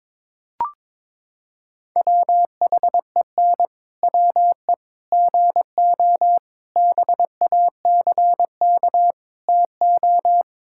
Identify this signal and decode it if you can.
Morse code, heard as a keyed tone, about 22 wpm, 700 Hz. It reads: WHEN WE GO BACK TO
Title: Pangram Challenge v2 - 22wpm